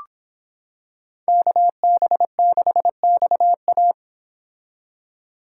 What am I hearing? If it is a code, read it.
Morse code, heard as a keyed tone, about 26 wpm, 700 Hz. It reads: KB6XA